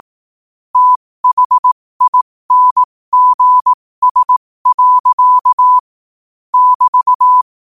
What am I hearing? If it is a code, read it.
Morse code, heard as a keyed tone, about 18 wpm, spaced as for 15 wpm, 1000 Hz. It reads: THINGS. <BT>
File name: qotdfiles / 15wpm